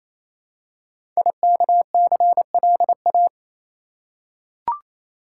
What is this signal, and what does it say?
Morse code, heard as a keyed tone, about 27 wpm, 700 Hz. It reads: IKCLA
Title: Code Groups - 5 Letter:Numbers - 28wpm